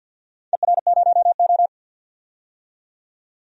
Morse code, heard as a keyed tone, about 50 wpm, 700 Hz. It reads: ER0O
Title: DXCC-100 Call Signs - 4 Characters - 50wpm